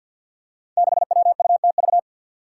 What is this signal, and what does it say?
Morse code, heard as a keyed tone, about 48 wpm, 700 Hz. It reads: 6WUTV
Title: Five Challenging Letter-Number Combinations 50wpm